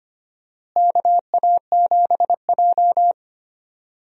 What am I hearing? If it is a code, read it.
Morse code, heard as a keyed tone, about 25 wpm, 700 Hz. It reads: KA7J
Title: USA Call Signs - 4 Characters - Repeated 2x - 25wpm